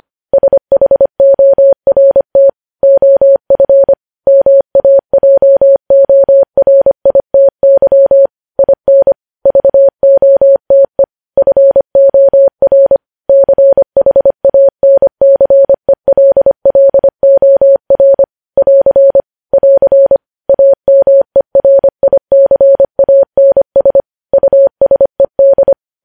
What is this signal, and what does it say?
Morse code, heard as a keyed tone, about 25 wpm, 550 Hz. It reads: SHORT OF MAJORITY IN VOTE FOR CHANCELLOR <AR> <AR> AMERICANS USED
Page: news headlines in morse code at 25 wpm